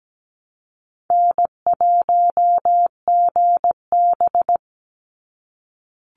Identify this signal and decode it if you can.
Morse code, heard as a keyed tone, about 17 wpm, 700 Hz. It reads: N1GB
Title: NA SOTA Chaser Callsigns 17wpm